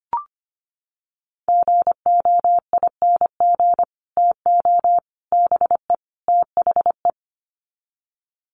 Morse code, heard as a keyed tone, about 25 wpm, 700 Hz. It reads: GOING TO BE THE
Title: Sets of 4 Words 25wpm